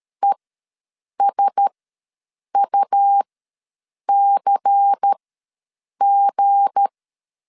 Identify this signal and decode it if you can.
Morse code, heard as a keyed tone, about 13 wpm, spaced as for 7 wpm, 800 Hz. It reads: ESUCG
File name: LEZIONI / 7 GF9Z / 7 G ws 12wpm 3140